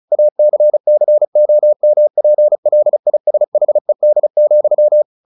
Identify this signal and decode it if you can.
Morse code, heard as a keyed tone, about 35 wpm, 600 Hz. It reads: ACCOMPLISHED,
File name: the old man and the sea 2 35wpm